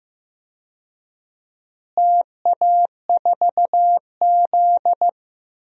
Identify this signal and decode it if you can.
Morse code, heard as a keyed tone, about 15 wpm, 700 Hz. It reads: TA4Z
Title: DXCC-100 Call Signs - 4 Characters - Repeated 2x - 15wpm